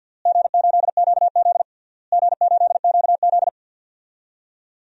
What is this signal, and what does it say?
Morse code, heard as a keyed tone, about 50 wpm, 700 Hz. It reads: G8QZ G8QZ
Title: DXCC-100 Call Signs - 4 Characters - Repeated 3x - 50wpm